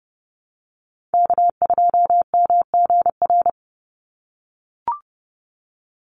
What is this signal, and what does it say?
Morse code, heard as a keyed tone, about 28 wpm, 700 Hz. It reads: K2MGR
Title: Call Signs - 30wpm